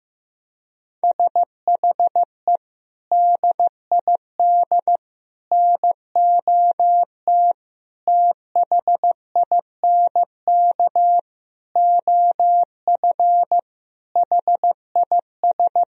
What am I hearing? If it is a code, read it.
Morse code, heard as a keyed tone, about 15 wpm, 700 Hz. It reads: SHE DID NOT THINK OF HIS
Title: Sentences from Top 1000 Words - Review Part 3 - 15wpm